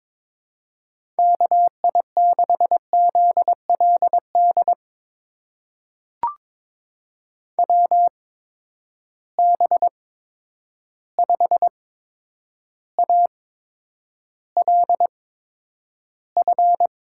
Morse code, heard as a keyed tone, about 22 wpm, spaced as for 9 wpm, 700 Hz. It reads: KI6ZLD WB5ALF
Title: USA Call Signs - 6 Characters - 8x Character Spacing - 22wpm